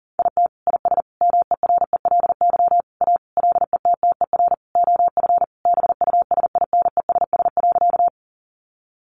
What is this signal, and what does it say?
Morse code, heard as a keyed tone, about 40 wpm, 700 Hz. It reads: IT IS MERELY A LETTER OF BUSINESS.
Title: Sentences from Top 1400 Words 40wpm